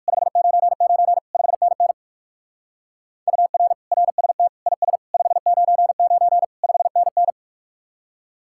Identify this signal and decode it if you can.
Morse code, heard as a keyed tone, about 60 wpm, 700 Hz. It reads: 599 5NN UR RST IS 599 5NN